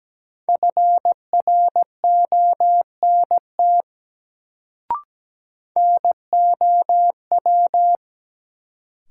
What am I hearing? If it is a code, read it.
Morse code, heard as a keyed tone, about 17 wpm, 700 Hz. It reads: FRONT NOW
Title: Sets of 2 Words - New Word Drill - 17wpm